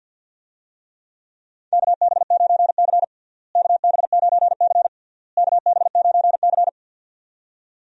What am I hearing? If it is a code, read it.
Morse code, heard as a keyed tone, about 47 wpm, 700 Hz. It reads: KB9C KB9C KB9C
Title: USA Call Signs - 4 Characters - Repeated 3x - 50wpm